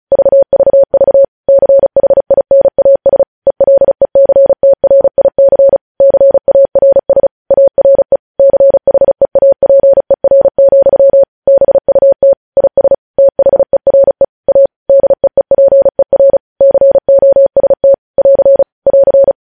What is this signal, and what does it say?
Morse code, heard as a keyed tone, about 36 wpm, 550 Hz. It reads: VVV CHINAS ELECTRIC CARS ARE CHEAPER, BUT IS THERE A DEEPER COST <AR> <AR>
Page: news headlines in morse code at 35 wpm